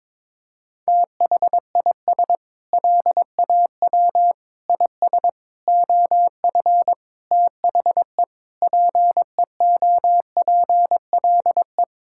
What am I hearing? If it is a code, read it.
Morse code, heard as a keyed tone, about 22 wpm, 700 Hz. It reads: THIS LAW IS OF THE PEOPLE
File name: Sentences with 9-Word Phrases - 22wpm